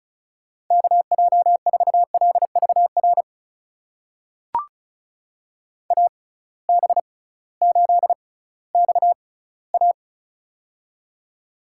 Morse code, heard as a keyed tone, about 36 wpm, spaced as for 19 wpm, 700 Hz. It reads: KJ4LVR AB8XA
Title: USA Call Signs - 6x Character Spacing - 35wpm